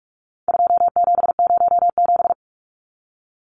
Morse code, heard as a keyed tone, about 44 wpm, 700 Hz. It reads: 2707
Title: Numbers - 4-Digit 45wpm